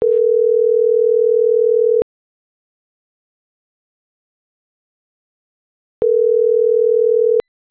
Ringback tone (North America), 440 + 480 Hz.